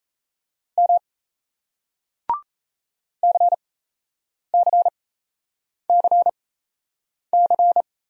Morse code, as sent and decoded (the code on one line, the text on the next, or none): -- / -.-. -.-. -.-. -.-.
M CCCC